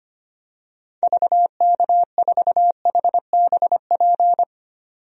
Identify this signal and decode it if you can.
Morse code, heard as a keyed tone, about 25 wpm, 690 Hz. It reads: VK4HBP
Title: DXCC-100 Call Signs - 6 Characters - 25wpm